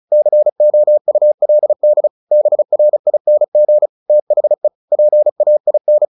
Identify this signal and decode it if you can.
Morse code, heard as a keyed tone, about 34 wpm, 600 Hz. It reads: COULD BRING THE PAIN
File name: the old man and the sea 2 35wpm